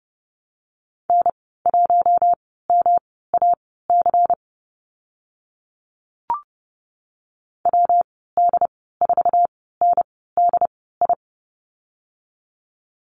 Morse code, heard as a keyed tone, about 30 wpm, spaced as for 17 wpm, 700 Hz. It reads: N1MAC WD4NDI
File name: Lesson 19g - Letter w - Call Signs 30wpm